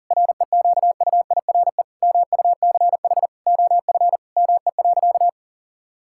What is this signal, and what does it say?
Morse code, heard as a keyed tone, about 40 wpm, 700 Hz. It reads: REQUIRE MUCH OF ME.
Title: Sentences from Top 1000 Words - Review Part 6 - 40wpm